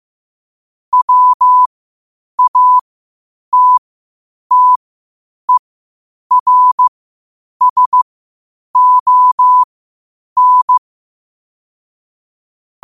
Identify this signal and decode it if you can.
Morse code, heard as a keyed tone, about 15 wpm, spaced as for 8 wpm, 1000 Hz. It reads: WATTERSON
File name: qotdfiles / 8wpm